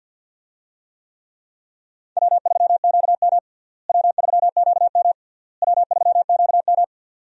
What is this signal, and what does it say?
Morse code, heard as a keyed tone, about 46 wpm, 700 Hz. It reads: W3QM W3QM W3QM
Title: USA Call Signs - 4 Characters - Repeated 3x - 50wpm